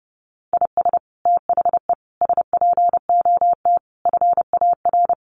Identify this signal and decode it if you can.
Morse code, heard as a keyed tone, about 30 wpm, 700 Hz. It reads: IS THE SPOT FAR
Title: Sentences from Top 1100 Words 30wpm